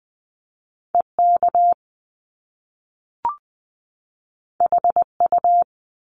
Morse code, heard as a keyed tone, about 20 wpm, 700 Hz. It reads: EK HU
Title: Top Two Letter Combinations in Words 20wpm